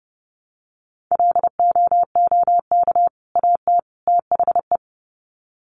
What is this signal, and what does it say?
Morse code, heard as a keyed tone, about 29 wpm, 700 Hz. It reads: LOOK AT THE
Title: Sets of 3 Words 30wpm